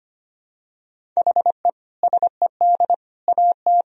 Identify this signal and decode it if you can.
Morse code, heard as a keyed tone, about 25 wpm, 700 Hz. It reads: HE SED AT
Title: Sentences from Top 300 Words and Abbreviations 25wpm